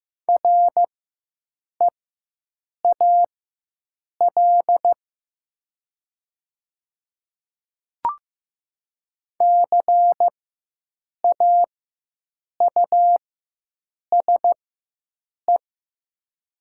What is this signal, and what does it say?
Morse code, heard as a keyed tone, about 15 wpm, spaced as for 7 wpm, 700 Hz. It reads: REAL CAUSE
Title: 4x Character Spacing - Top 300 Words - 15wpm